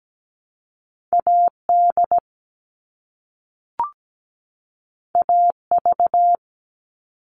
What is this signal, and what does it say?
Morse code, heard as a keyed tone, about 17 wpm, 700 Hz. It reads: AD AV
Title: All Cut Numbers - 2-Digit 17wpm